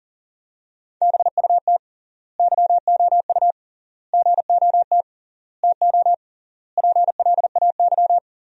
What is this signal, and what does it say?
Morse code, heard as a keyed tone, about 40 wpm, 700 Hz. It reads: BUT YOU GOT TO PLAY
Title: 3x Word-Spacing - Sentences from Top 300 Words - 40wpm